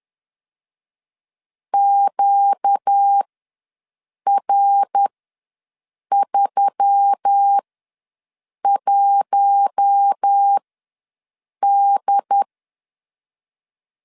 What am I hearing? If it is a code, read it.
Morse code, heard as a keyed tone, about 10 wpm, spaced as for 6 wpm, 800 Hz. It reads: QR31D
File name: LEZIONI / 5 LBVC / 5 LB ws 10wpm 3616